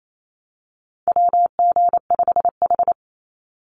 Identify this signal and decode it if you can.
Morse code, heard as a keyed tone, about 28 wpm, 700 Hz. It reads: WG5H
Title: Call Signs - 28wpm